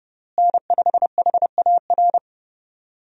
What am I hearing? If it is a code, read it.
Morse code, heard as a keyed tone, about 32 wpm, 700 Hz. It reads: N5HAR
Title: USA Call Signs - 5 Characters - Repeated 3x - 30wpm